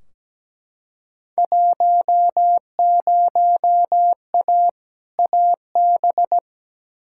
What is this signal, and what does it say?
Morse code, heard as a keyed tone, about 17 wpm, 700 Hz. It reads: 10A AB
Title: Field Day Exchanges Encore - 17wpm